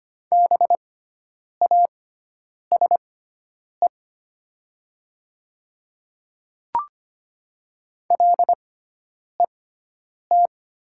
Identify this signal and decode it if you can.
Morse code, heard as a keyed tone, about 25 wpm, spaced as for 9 wpm, 700 Hz. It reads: BASE LET
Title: 6x Character Spacing - Top 500 Words - 25wpm